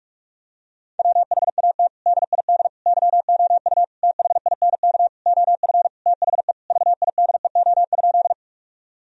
Morse code, heard as a keyed tone, about 46 wpm, 700 Hz. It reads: WHAT DID YOU THINK OF THE VIDEO?
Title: Sentences from Top 1000 Words - Review Part 4 - 45wpm